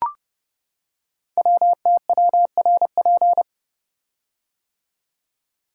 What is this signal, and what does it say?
Morse code, heard as a keyed tone, about 30 wpm, 700 Hz. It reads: WTWRP
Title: Code Groups - 5 Letters - 30wpm